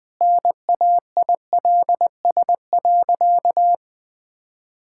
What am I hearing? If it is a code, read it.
Morse code, heard as a keyed tone, about 20 wpm, 700 Hz. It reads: NAILS.
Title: Easy Sentences with Hints - 20wpm